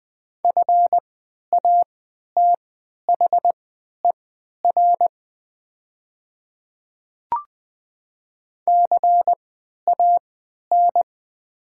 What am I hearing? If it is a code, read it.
Morse code, heard as a keyed tone, about 20 wpm, spaced as for 11 wpm, 690 Hz. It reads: FATHER CAN
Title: 3x Character Spacing - Top 300 Words - 20wpm